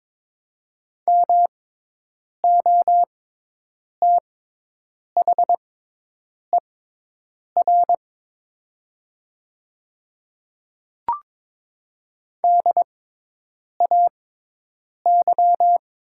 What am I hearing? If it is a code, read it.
Morse code, heard as a keyed tone, about 22 wpm, spaced as for 8 wpm, 700 Hz. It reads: MOTHER DAY
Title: 6x Character Spacing - Top 300 Words - 22wpm